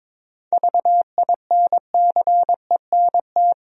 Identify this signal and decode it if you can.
Morse code, heard as a keyed tone, about 22 wpm, 700 Hz. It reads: VINCENT
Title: USA Common Names - 22wpm